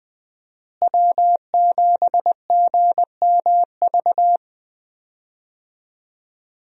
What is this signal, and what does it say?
Morse code, heard as a keyed tone, about 20 wpm, 700 Hz. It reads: W7GMV
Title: USA Call Signs - 5 Characters - Repeated 2x - 20wpm